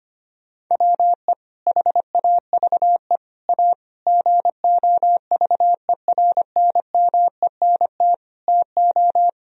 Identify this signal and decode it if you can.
Morse code, heard as a keyed tone, about 25 wpm, 700 Hz. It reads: WE HAVE A GOVERNMENT TO
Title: Sentences from Top 1000 Words - Review Part 1 - 25wpm